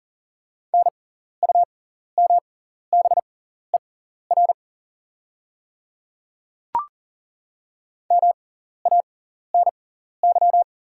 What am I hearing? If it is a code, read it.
Morse code, heard as a keyed tone, about 40 wpm, spaced as for 14 wpm, 690 Hz. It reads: NUMBER MANY
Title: 6x Character Spacing - Top 100 Words - 40wpm